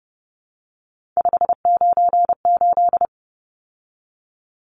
Morse code, heard as a keyed tone, about 30 wpm, 700 Hz. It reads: 598
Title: Numbers - 3-Digit 30wpm